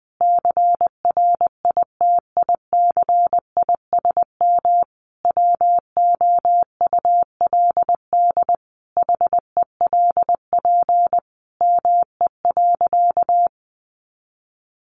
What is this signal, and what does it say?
Morse code, heard as a keyed tone, about 20 wpm, 700 Hz. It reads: CRITICISM WOULD HELP ME.
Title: Sentences from Top 1300 Words 20wpm